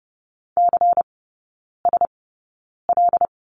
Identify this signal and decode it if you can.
Morse code, heard as a keyed tone, about 30 wpm, spaced as for 9 wpm, 690 Hz. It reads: CSL
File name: Lesson 16e - Letter c - Three Characters 30wpm